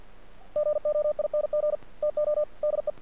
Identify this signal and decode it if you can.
Morse code, heard as a keyed tone, about 50 wpm, 600 Hz. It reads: GOING TO DE
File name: archived / CRSnet